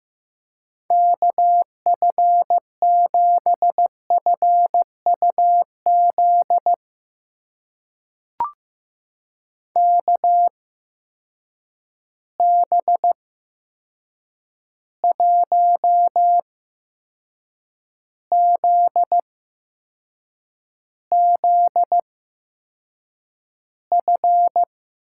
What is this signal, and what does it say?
Morse code, heard as a keyed tone, about 15 wpm, spaced as for 6 wpm, 700 Hz. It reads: KF7FUZ KB1ZZF
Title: USA Call Signs - 6 Characters - 8x Character Spacing - 15wpm